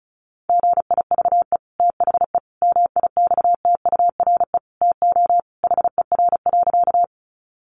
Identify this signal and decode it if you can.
Morse code, heard as a keyed tone, about 34 wpm, 700 Hz. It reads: GIVE THE MIXTURE TO HER.
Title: Sentences from Top 1500 Words 35wpm